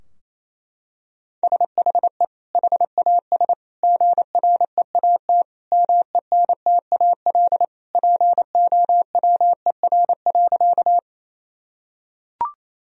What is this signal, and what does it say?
Morse code, heard as a keyed tone, about 28 wpm, 700 Hz. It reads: SHE HAS GREAT MENTAL POWER.
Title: Sentences from Top 1000 Words - Review Part 4 - 28wpm